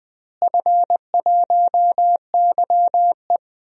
Morse code, heard as a keyed tone, about 21 wpm, 700 Hz. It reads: F1YE
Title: DXCC-100 Call Signs - 4 Characters - Repeated 3x - 20wpm